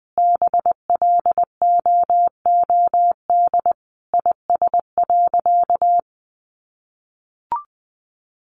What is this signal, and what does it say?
Morse code, heard as a keyed tone, about 20 wpm, 700 Hz. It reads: BLOOD IS.
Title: Sentences from Top 1100 Words 20wpm